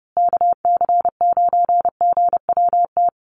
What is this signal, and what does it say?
Morse code, heard as a keyed tone, about 30 wpm, 700 Hz. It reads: KC9GWT
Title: USA Call Signs Repeated 3x 30wpm